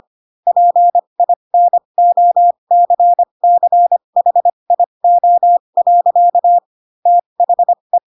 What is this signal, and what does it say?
Morse code, heard as a keyed tone, about 25 wpm, spaced as for 22 wpm, 700 Hz. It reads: PINOCCHIO. THE
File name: Pinocchio / Pinocchio ch3 18 25 ik1hge